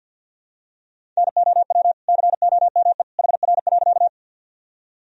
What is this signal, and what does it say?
Morse code, heard as a keyed tone, about 50 wpm, 700 Hz. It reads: NOW COME HR.
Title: Sentences from Top 200 Words and Abbreviations 50wpm